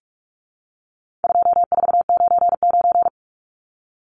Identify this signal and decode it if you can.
Morse code, heard as a keyed tone, about 44 wpm, 700 Hz. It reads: 2499
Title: Numbers - 4-Digit 45wpm